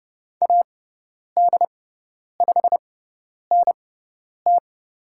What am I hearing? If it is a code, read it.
Morse code, heard as a keyed tone, about 30 wpm, spaced as for 10 wpm, 700 Hz. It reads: AD5NT
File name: Lesson 16g - Letter c - Call Signs 30wpm